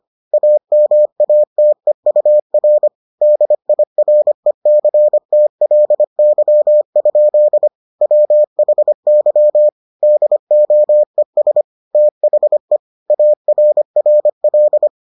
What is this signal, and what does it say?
Morse code, heard as a keyed tone, about 25 wpm, 600 Hz. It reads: AMATEUR DIRECTLY? WHY DOES THE ARRL